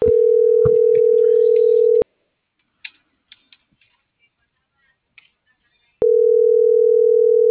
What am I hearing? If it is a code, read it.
Ringback tone (North America), 440 + 480 Hz.